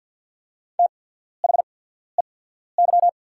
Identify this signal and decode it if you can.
Morse code, heard as a keyed tone, about 48 wpm, spaced as for 14 wpm, 690 Hz. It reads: THEY
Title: 8x Character Spacing - Top 300 Words - 50wpm